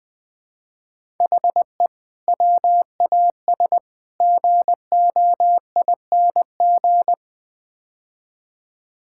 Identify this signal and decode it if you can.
Morse code, heard as a keyed tone, about 21 wpm, 700 Hz. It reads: HE WAS GOING
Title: Build A Phrase - 5-Word Phrases - 20wpm